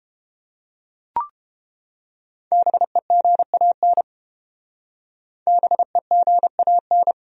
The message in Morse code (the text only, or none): BEGAN BEGAN